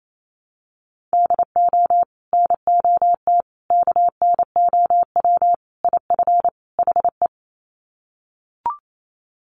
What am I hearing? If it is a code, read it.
Morse code, heard as a keyed tone, about 28 wpm, 700 Hz. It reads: DO NOT KNOW IF HE
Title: Sets of 5 Words Encore2 28wpm